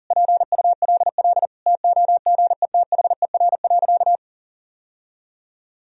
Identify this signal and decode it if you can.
Morse code, heard as a keyed tone, about 40 wpm, 700 Hz. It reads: PULL TOGETHER.
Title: Sentences from Top 1000 Words - Review Part 2 - 40wpm